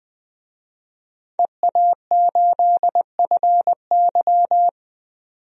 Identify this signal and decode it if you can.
Morse code, heard as a keyed tone, about 20 wpm, 700 Hz. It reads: EA8FY
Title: DXCC-100 Call Signs - 5 Characters - 20wpm